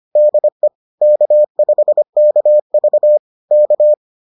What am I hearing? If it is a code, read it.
Morse code, heard as a keyed tone, about 25 wpm, 600 Hz. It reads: DE K5KV K